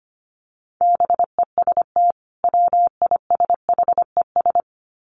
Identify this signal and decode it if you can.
Morse code, heard as a keyed tone, about 25 wpm, 700 Hz. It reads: BEST WISHES
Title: Common QSO Phrases 25wpm